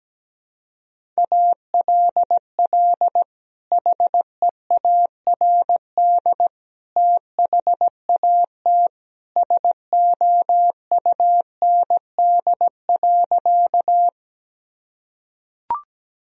Morse code, heard as a keyed tone, about 17 wpm, 700 Hz. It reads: ALL HEARD THAT SOUND.
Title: Sentences from Top 1000 Words - Review Part 1 - 17wpm